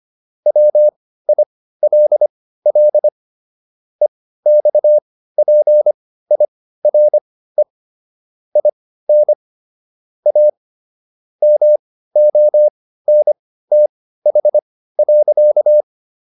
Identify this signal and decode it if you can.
Morse code, heard as a keyed tone, about 25 wpm, spaced as for 15 wpm, 600 Hz. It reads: WILL EXPIRE IN A MONTH.